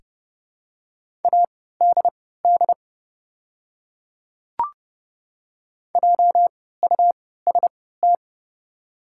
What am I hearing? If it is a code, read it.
Morse code, heard as a keyed tone, about 29 wpm, spaced as for 17 wpm, 700 Hz. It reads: ADD JUST
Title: 3x Character Spacing - Top 500 Words - 30wpm